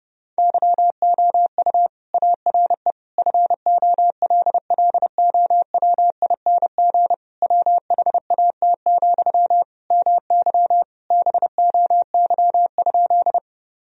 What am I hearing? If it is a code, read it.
Morse code, heard as a keyed tone, about 32 wpm, 700 Hz. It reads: YOU ARE FOLLOWING WHAT, MY BOY?
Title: Sentences from Top 1000 Words - Review Part 4 - 30wpm